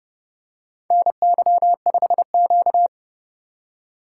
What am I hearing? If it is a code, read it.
Morse code, heard as a keyed tone, about 30 wpm, 700 Hz. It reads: NY5Q
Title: USA Call Signs - 4 Characters - 4x Character Spacing - 30wpm